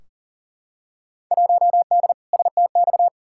Morse code, heard as a keyed tone, about 40 wpm, 700 Hz. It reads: 1D STX
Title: Field Day Exchanges - 40wpm